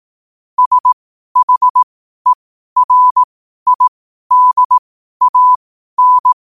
Morse code, heard as a keyed tone, about 18 wpm, spaced as for 12 wpm, 1000 Hz. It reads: SHERIDAN
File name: qotdfiles / 12wpm